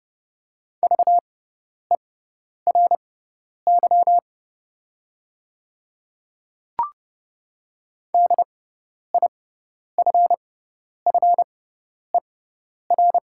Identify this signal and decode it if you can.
Morse code, heard as a keyed tone, about 30 wpm, spaced as for 10 wpm, 700 Hz. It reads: VERY DIFFER
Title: 6x Character Spacing - Top 300 Words - 30wpm